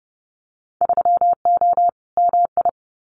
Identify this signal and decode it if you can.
Morse code, heard as a keyed tone, about 30 wpm, 700 Hz. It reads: 3O MI
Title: Winter Field Day Exchanges Encore 30wpm